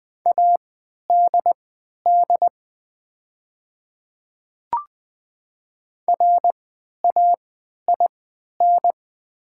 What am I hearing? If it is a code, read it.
Morse code, heard as a keyed tone, about 20 wpm, spaced as for 11 wpm, 700 Hz. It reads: ADD RAIN
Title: 3x Character Spacing - Top 500 Words - 20wpm